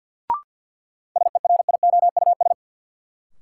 Morse code, heard as a keyed tone, about 46 wpm, 700 Hz. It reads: SERIOUS
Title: Sentences from Top 1100 Words - New Word Drill Alt - 50wpm